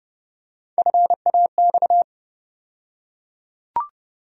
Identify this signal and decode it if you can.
Morse code, heard as a keyed tone, about 31 wpm, 700 Hz. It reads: FAX
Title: USA Call Sign Postfixes - 3 Letters 30wpm